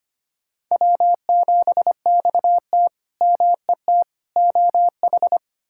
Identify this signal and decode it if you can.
Morse code, heard as a keyed tone, about 25 wpm, 700 Hz. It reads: W7XT MET OH
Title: K1USN SST QSO Practice - 25wpm